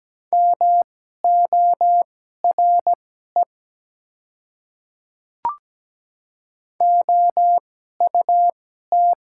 Morse code, heard as a keyed tone, about 17 wpm, spaced as for 12 wpm, 690 Hz. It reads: MORE OUT
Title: 2x Character Spacing - Top 100 Words - 17wpm